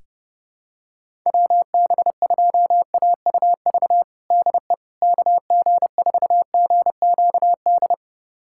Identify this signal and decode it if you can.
Morse code, heard as a keyed tone, about 30 wpm, 700 Hz. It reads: WB2AUV DE KG4GQD
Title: USA Callsign DE Callsign - 6 Characters - 30wpm